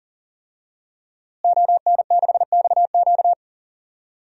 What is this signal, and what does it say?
Morse code, heard as a keyed tone, about 41 wpm, 700 Hz. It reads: ON6XQ
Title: DXCC-100 Call Signs - 5 Characters - Repeated 2x - 40wpm